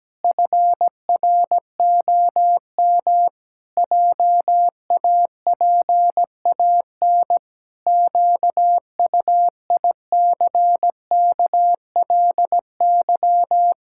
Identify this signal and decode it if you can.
Morse code, heard as a keyed tone, about 17 wpm, 700 Hz. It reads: FROM JAPAN QUICKLY
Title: Pangram Challenge v2 - 17wpm